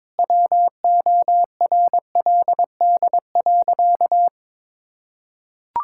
Morse code, heard as a keyed tone, about 22 wpm, 700 Hz. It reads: WORLD.
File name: Sentences with 9-Word Phrases - 22wpm